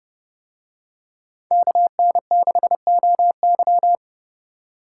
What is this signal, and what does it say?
Morse code, heard as a keyed tone, about 28 wpm, 700 Hz. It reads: KN6OY